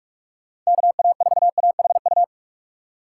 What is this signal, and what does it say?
Morse code, heard as a keyed tone, about 60 wpm, spaced as for 52 wpm, 700 Hz. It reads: KA4AHU
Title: Speed Racing - USA 2x3 Call Signs 45wpm